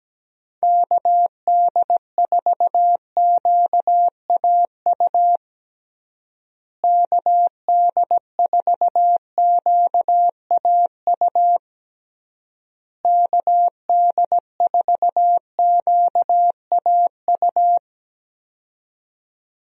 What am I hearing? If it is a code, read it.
Morse code, heard as a keyed tone, about 17 wpm, 700 Hz. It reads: KD4QAU KD4QAU KD4QAU
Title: USA Call Signs - 6 Characters - Repeated 3x - 17wpm